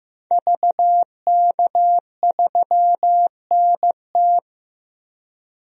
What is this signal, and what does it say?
Morse code, heard as a keyed tone, about 15 wpm, 700 Hz. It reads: VK3NT
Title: DXCC-100 Call Signs - 5 Characters - 15wpm